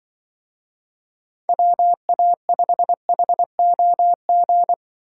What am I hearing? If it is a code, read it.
Morse code, heard as a keyed tone, about 25 wpm, 700 Hz. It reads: WA5HOG
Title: Speed Racing - USA 2x3 Call Signs 20wpm